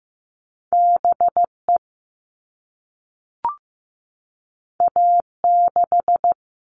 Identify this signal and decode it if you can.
Morse code, heard as a keyed tone, about 15 wpm, 700 Hz. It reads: BE A6
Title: All Cut Numbers - 2-Digit 15wpm